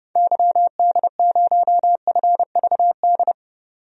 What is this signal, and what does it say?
Morse code, heard as a keyed tone, about 31 wpm, 690 Hz. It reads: YD0FVD
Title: DXCC-100 Call Signs - 6 Characters - 30wpm